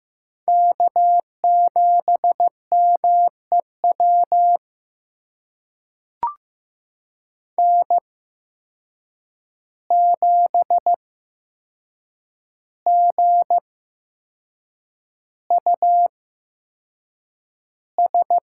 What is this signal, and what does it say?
Morse code, heard as a keyed tone, about 15 wpm, spaced as for 6 wpm, 700 Hz. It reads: K7MEW N7GUS
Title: USA Call Signs - 5 Characters - 8x Character Spacing - 15wpm